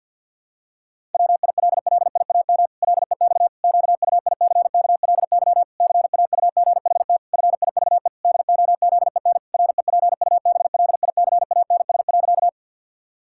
Morse code, heard as a keyed tone, about 50 wpm, 700 Hz. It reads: WILLIAM LEX QUICKLY CAUGHT FIVE DOZEN REPUBLICANS.
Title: Pangram Challenge v2 - 50wpm